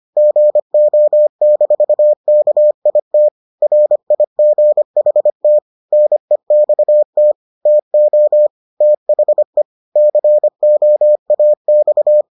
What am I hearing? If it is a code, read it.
Morse code, heard as a keyed tone, about 25 wpm, 600 Hz. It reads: GO-KIT RIGHT NEXT TO THE COAX